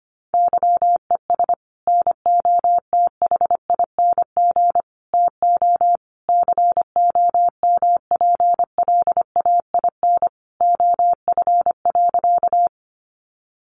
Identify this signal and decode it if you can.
Morse code, heard as a keyed tone, about 25 wpm, 700 Hz. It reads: YES NOTHING TO COMPLAIN OF.
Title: Sentences from Top 1300 Words 25wpm